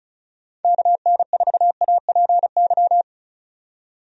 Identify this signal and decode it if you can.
Morse code, heard as a keyed tone, about 36 wpm, 700 Hz. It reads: KN4APY
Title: USA Call Signs - 6 Characters - 35wpm